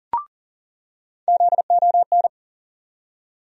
Morse code, heard as a keyed tone, about 39 wpm, 700 Hz. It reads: ZON
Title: USA Call Sign Postfixes - 3 Letters 40wpm